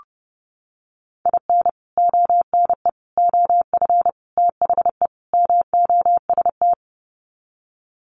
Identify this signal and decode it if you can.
Morse code, heard as a keyed tone, about 30 wpm, 700 Hz. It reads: IN ONE OF THE MOST
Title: Sets of 5 Words Encore 30wpm